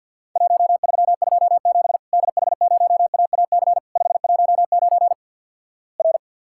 Morse code, heard as a keyed tone, about 50 wpm, 700 Hz. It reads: 1327 DH0AAC 519 R
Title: Worked All Europe DX Contest - QTC Exchange with Repeats - 50wpm